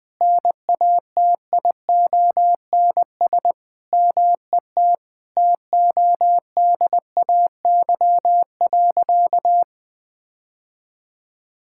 Morse code, heard as a keyed tone, about 20 wpm, 700 Hz. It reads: NATIONS MET TODAY.
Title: Sentences with 3-Word Phrases - 20wpm